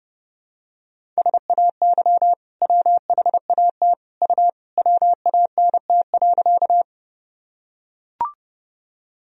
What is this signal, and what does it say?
Morse code, heard as a keyed tone, about 30 wpm, 700 Hz. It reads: SAY WHAT U WANT.
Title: Sentences from Top 200 Words and Abbreviations 30wpm